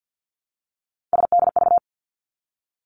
Morse code, heard as a keyed tone, about 50 wpm, 700 Hz. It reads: SDV
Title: Three Challenging Letter-Number Combinations 50wpm